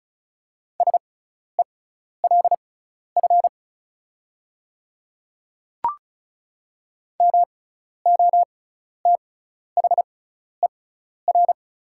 Morse code, heard as a keyed tone, about 36 wpm, spaced as for 12 wpm, 690 Hz. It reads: SELF MOTHER
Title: 6x Character Spacing - Top 300 Words - 35wpm